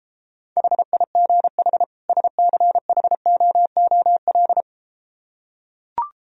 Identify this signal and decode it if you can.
Morse code, heard as a keyed tone, about 33 wpm, 700 Hz. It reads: HIGH SCHOOL